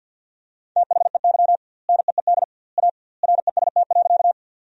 Morse code, heard as a keyed tone, about 52 wpm, 700 Hz. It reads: THEY NEED A REST.
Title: 2x Word-Spacing - Sentences from Top 500 Words - 50wpm